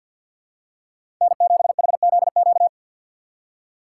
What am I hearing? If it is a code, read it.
Morse code, heard as a keyed tone, about 54 wpm, 700 Hz. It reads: N7HZQ
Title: USA Call Signs - 5 Characters - 4x Character Spacing - 50wpm